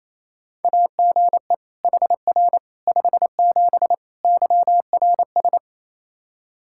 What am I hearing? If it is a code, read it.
Morse code, heard as a keyed tone, about 28 wpm, 700 Hz. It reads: AGE HR 57 YRS
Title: Practice QSO - Exchange 3 - More Info - 28wpm